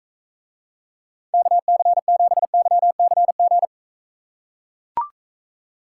Morse code, heard as a keyed tone, about 43 wpm, 700 Hz. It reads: KC7YCG